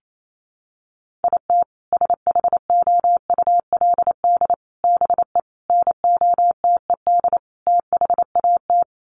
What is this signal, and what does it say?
Morse code, heard as a keyed tone, about 28 wpm, 700 Hz. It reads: IT SHOULD BE NOTED THAT
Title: Sets of 5 Words 28wpm